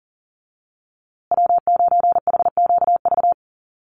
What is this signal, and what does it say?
Morse code, heard as a keyed tone, about 40 wpm, 700 Hz. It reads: W9HQV
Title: USA Call Signs Repeated 2x 40wpm